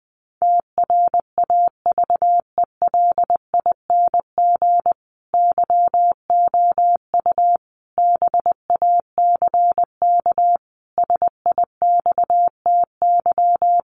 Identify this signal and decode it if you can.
Morse code, heard as a keyed tone, about 20 wpm, 700 Hz. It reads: TRAVELING YOU BACK SIXTY
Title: A Sound Of Thunder 20wpm